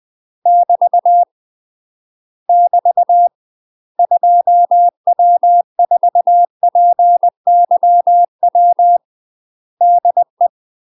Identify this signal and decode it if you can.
Morse code, heard as a keyed tone, about 20 wpm, 700 Hz. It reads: <BT> <BT> 2W4PYW DE
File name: UK MASTER QSOs TRAINING / 20WPM / UK MASTER QSOs 0015